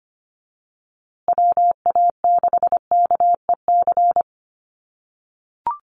Morse code, heard as a keyed tone, about 25 wpm, 700 Hz. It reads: WA6KEC
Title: USA Call Signs Repeated 2x 25wpm